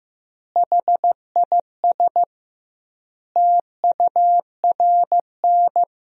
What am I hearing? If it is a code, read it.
Morse code, heard as a keyed tone, about 16 wpm, 700 Hz. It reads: HIS TURN
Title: 2x Word-Spacing - Sentences from Top 200 Words - 15wpm